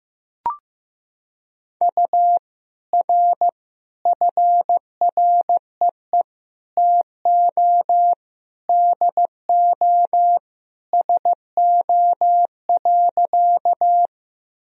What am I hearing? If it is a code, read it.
Morse code, heard as a keyed tone, about 15 wpm, 700 Hz. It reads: U R FREE TO DO SO.
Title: Sentences from Top 500 Words and Abbreviations 15wpm